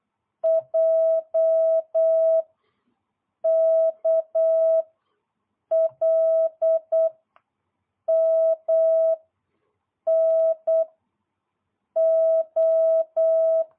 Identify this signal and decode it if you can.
Morse code, heard as a keyed tone, about 8 wpm, 650 Hz. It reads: J K L M N O